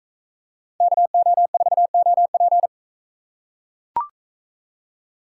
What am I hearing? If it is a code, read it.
Morse code, heard as a keyed tone, about 40 wpm, 700 Hz. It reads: KO4OP